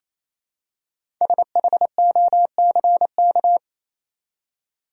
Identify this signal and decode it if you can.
Morse code, heard as a keyed tone, about 28 wpm, 700 Hz. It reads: SHOCK
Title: Five Letter Words - 28wpm